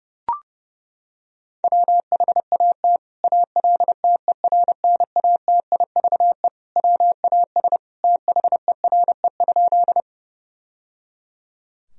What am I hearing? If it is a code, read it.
Morse code, heard as a keyed tone, about 30 wpm, 700 Hz. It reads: WHAT ALTERNATIVE WAS THERE?
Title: Sentences from Top 1000 Words - Review Part 4 - 30wpm